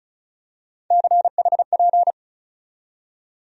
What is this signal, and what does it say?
Morse code, heard as a keyed tone, about 36 wpm, 700 Hz. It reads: CHP
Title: USA Call Sign Postfixes - 3 Letters 35wpm